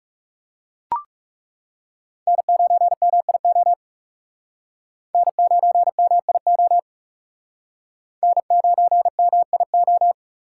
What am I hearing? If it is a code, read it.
Morse code, heard as a keyed tone, about 37 wpm, 700 Hz. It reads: N9MIO N9MIO N9MIO
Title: Speed Racing - USA 1x3 Call Signs 30wpm